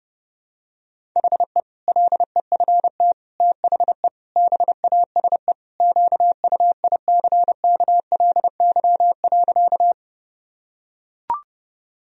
Morse code, heard as a keyed tone, about 32 wpm, 700 Hz. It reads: HE LEFT THE BASE QUICKLY.
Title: Sentences from Top 1000 Words - Review Part 1 - 30wpm